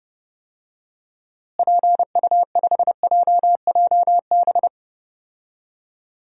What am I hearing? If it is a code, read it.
Morse code, heard as a keyed tone, about 32 wpm, 700 Hz. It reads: PU5JJB
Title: DXCC-100 Call Signs - 6 Characters - Repeated 3x - 30wpm